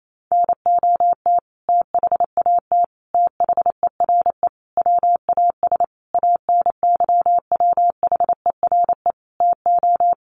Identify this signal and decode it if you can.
Morse code, heard as a keyed tone, about 28 wpm, 700 Hz. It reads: NOT THAT THERE WAS ANYWHERE TO
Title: Sentences from Top 1400 Words 28wpm